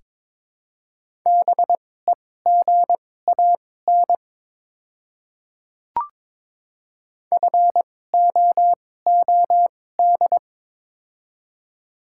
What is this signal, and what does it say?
Morse code, heard as a keyed tone, about 22 wpm, spaced as for 16 wpm, 700 Hz. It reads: BEGAN FOOD
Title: 2x Character Spacing - Top 300 Words - 22wpm